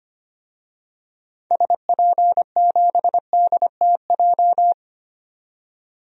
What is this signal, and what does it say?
Morse code, heard as a keyed tone, about 25 wpm, 700 Hz. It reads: SP7DTJ
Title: DXCC-100 Call Signs - 6 Characters - 25wpm